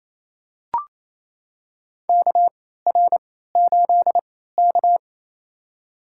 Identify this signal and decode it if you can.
Morse code, heard as a keyed tone, about 28 wpm, spaced as for 16 wpm, 700 Hz. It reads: KR8K
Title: USA Call Signs - 4 Characters - 3x Character Spacing - 28wpm